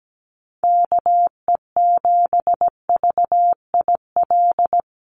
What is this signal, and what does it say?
Morse code, heard as a keyed tone, about 17 wpm, 700 Hz. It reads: KE7VIL
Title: USA Call Signs 17wpm